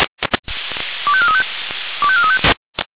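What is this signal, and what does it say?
Five-tone selective call (ZVEI), heard as quick runs of five tones, tones of about 70 ms. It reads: 25435 25435
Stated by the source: ZVEI five-tone selective call received on an SDR.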